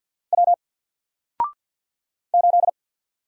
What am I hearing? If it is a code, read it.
Morse code, heard as a keyed tone, about 50 wpm, 700 Hz. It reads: W 8
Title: Single Character (Letter:Number:Punctuation:Prosign) 50wpm